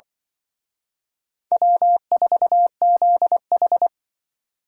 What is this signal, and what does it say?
Morse code, heard as a keyed tone, about 24 wpm, 700 Hz. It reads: W4ZH
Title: Speed Racing - USA 1x2 Call Signs 20wpm